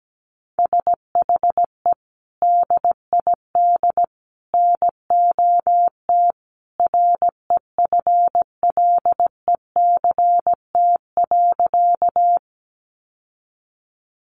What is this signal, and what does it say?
Morse code, heard as a keyed tone, about 17 wpm, 700 Hz. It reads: SHE DID NOT REFLECT.
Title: Sentences from Top 1500 Words 17wpm